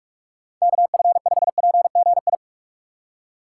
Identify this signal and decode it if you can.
Morse code, heard as a keyed tone, about 44 wpm, 700 Hz. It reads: KF5PGI
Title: USA Call Signs - 6 Characters - 2x Character Spacing - 45wpm